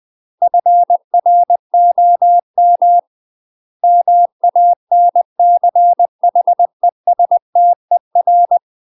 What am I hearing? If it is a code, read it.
Morse code, heard as a keyed tone, about 20 wpm, 700 Hz. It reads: FROM MANCHESTER
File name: UK MASTER QSOs TRAINING / 20WPM / UK MASTER QSOs 0003